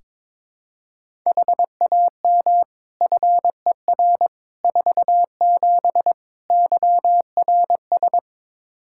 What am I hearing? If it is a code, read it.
Morse code, heard as a keyed tone, about 22 wpm, 700 Hz. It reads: HAM FER 47 YRS
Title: K1USN SST QSO Practice 22wpm